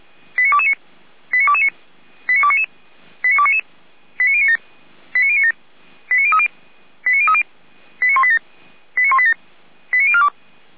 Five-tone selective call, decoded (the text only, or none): 89229 89229 89299 89299 89097 89097 89030 89030 89187 89187 89042